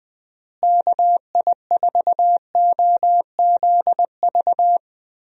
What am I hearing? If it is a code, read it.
Morse code, heard as a keyed tone, about 21 wpm, 700 Hz. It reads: KI4OZV
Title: USA Call Signs - 6 Characters - Repeated 3x - 20wpm